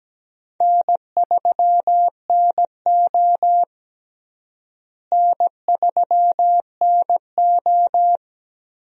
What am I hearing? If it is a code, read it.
Morse code, heard as a keyed tone, about 17 wpm, 700 Hz. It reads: N3NO N3NO
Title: USA Call Signs - 4 Characters - Repeated 2x - 17wpm